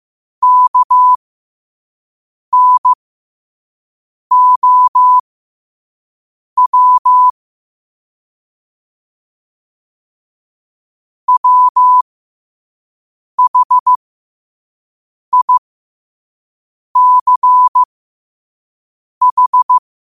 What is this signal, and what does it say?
Morse code, heard as a keyed tone, about 15 wpm, spaced as for 5 wpm, 1000 Hz. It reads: KNOW WHICH
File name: qotdfiles / 5wpm